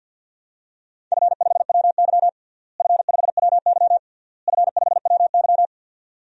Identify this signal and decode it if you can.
Morse code, heard as a keyed tone, about 52 wpm, 700 Hz. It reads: F5WY F5WY F5WY
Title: DXCC-100 Call Signs - 4 Characters - Repeated 3x - 50wpm